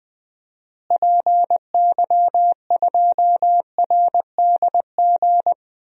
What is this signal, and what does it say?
Morse code, heard as a keyed tone, about 19 wpm, 700 Hz. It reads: PY2RDG
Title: DXCC-100 Call Signs - 6 Characters - Repeated 3x - 20wpm